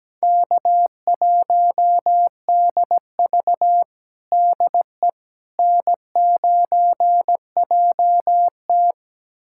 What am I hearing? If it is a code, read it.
Morse code, heard as a keyed tone, about 17 wpm, 700 Hz. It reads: K1DV DE N9JT
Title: USA Callsign DE Callsign - 4 Characters - 17wpm